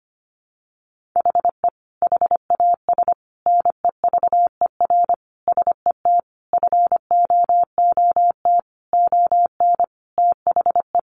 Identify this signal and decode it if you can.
Morse code, heard as a keyed tone, about 25 wpm, 700 Hz. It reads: HE HAS NEVER SET FOOT ON THE
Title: Sentences from Top 1500 Words 25wpm